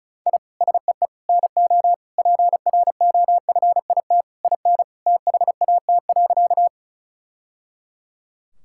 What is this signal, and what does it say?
Morse code, heard as a keyed tone, about 34 wpm, 700 Hz. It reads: I SEE NO PROFIT IN THAT.
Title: Sentences from Top 1000 Words - Review Part 5 - 35wpm